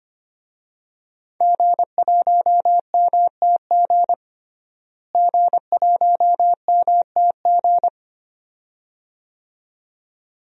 Morse code, heard as a keyed tone, about 25 wpm, 700 Hz. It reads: G1MTG G1MTG
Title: DXCC-100 Call Signs - 5 Characters - Repeated 2x - 25wpm